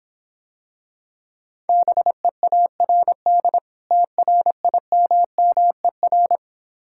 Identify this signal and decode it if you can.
Morse code, heard as a keyed tone, about 26 wpm, 700 Hz. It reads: BEARD TRIMMER